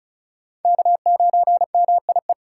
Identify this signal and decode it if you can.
Morse code, heard as a keyed tone, about 36 wpm, 700 Hz. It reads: K9MIE
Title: USA Call Signs - 4x Character Spacing - 35wpm